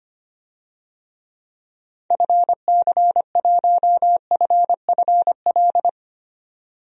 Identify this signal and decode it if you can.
Morse code, heard as a keyed tone, about 25 wpm, 700 Hz. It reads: FC1FFL
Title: DXCC-100 Call Signs - 6 Characters - 25wpm